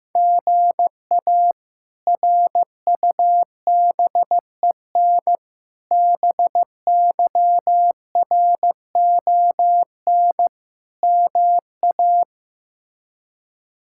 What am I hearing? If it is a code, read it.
Morse code, heard as a keyed tone, about 15 wpm, 700 Hz. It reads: GA RUBEN BYRON MA
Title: K1USN SST QSO Practice - 15wpm